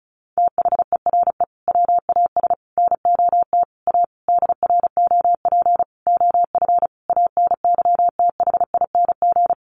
Morse code, heard as a keyed tone, about 34 wpm, 700 Hz. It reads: THERE WAS NOT A DROP OF ANYTHING
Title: Sentences from Top 1500 Words 35wpm